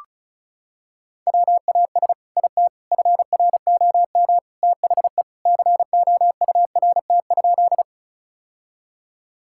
Morse code, heard as a keyed tone, about 34 wpm, 700 Hz. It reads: WAS IT FROM THE COURT?
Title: Sentences from Top 1600 Words 35wpm